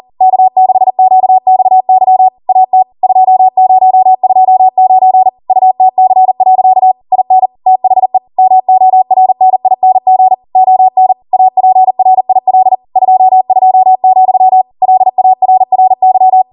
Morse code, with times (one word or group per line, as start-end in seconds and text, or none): 0.2-2.3 K6QXY
2.5-2.8 AT
3.0-5.3 2029
5.5-6.9 UTC.
7.1-7.5 IN
7.7-8.2 THE
8.4-10.3 MORNING
10.6-11.1 ON
11.3-12.8 APRIL
13.0-14.6 22,
14.8-16.4 LARRY